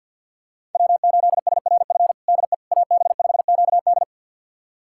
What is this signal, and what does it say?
Morse code, heard as a keyed tone, about 52 wpm, 700 Hz. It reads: W8SRF DE AB5QD
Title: USA Callsign DE Callsign - 5 Characters - 50wpm